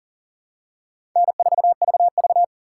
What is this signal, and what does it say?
Morse code, heard as a keyed tone, about 40 wpm, 690 Hz. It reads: N4VV
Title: USA Call Signs - 2x Character Spacing - 40wpm